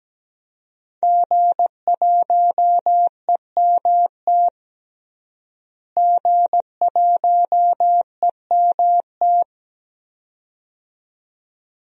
Morse code, heard as a keyed tone, about 17 wpm, 700 Hz. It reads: G1EMT G1EMT
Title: DXCC-100 Call Signs - 5 Characters - Repeated 2x - 17wpm